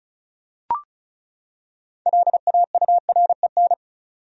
Morse code, heard as a keyed tone, about 35 wpm, 700 Hz. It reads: LAUREN
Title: USA Common Names - 35wpm